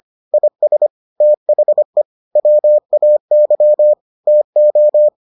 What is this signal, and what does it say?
Morse code, heard as a keyed tone, about 25 wpm, 600 Hz. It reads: IS THE WAY TO